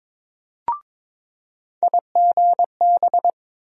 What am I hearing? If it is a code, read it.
Morse code, heard as a keyed tone, about 22 wpm, 700 Hz. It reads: IGB
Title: USA Call Sign Postfixes - 3 Letters 22wpm